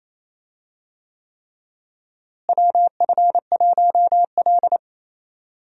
Morse code, heard as a keyed tone, about 28 wpm, 700 Hz. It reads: WF1L
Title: USA Call Signs - 4 Characters - Repeated 3x - 28wpm